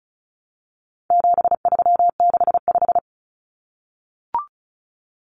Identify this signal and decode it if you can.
Morse code, heard as a keyed tone, about 34 wpm, 700 Hz. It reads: 7365
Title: Common Cut Numbers - 4-Digit 35wpm